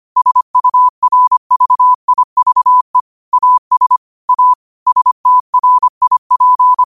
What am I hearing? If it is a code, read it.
Morse code, heard as a keyed tone, about 25 wpm, 1000 Hz. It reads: SURVIVE AS A STRIP
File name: qotdfiles / 25wpm